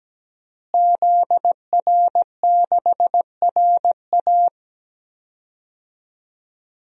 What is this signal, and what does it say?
Morse code, heard as a keyed tone, about 17 wpm, 700 Hz. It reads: ZR6RA
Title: DXCC-100 Call Signs - 5 Characters - Repeated 3x - 17wpm